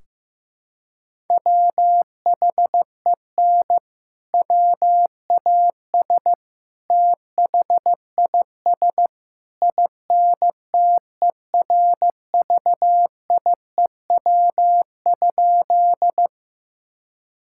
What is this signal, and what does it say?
Morse code, heard as a keyed tone, about 15 wpm, 700 Hz. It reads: WHEN WAS THIS INTERVIEW?
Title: Sentences from Top 1600 Words 15wpm